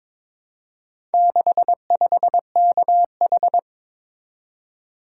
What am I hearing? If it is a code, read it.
Morse code, heard as a keyed tone, about 22 wpm, 700 Hz. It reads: 65KH
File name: practice / 22wpm / Four Letter-Number Combinations - 22wpm